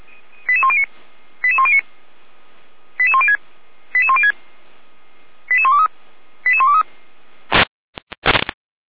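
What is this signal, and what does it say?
Five-tone selective call (ZVEI), heard as quick runs of five tones, tones of about 70 ms. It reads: 88109 88109 88106 88106 88123 88123
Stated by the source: ZVEI five-tone selective call received by SDR, from the Feuerwehr Orlamünde (fire brigade) audio server.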